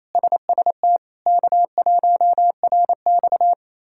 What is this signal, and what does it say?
Morse code, heard as a keyed tone, about 27 wpm, 700 Hz. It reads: SST K1RX
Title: K1USN SST QSO Practice - 28wpm